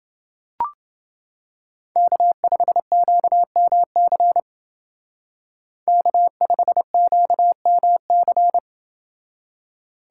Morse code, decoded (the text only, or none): K5QMC K5QMC